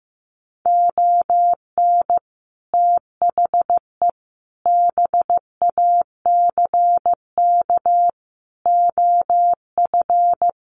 Morse code, heard as a keyed tone, about 15 wpm, 700 Hz. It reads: ON THE BACK OF
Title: Sets of 4 Words 15wpm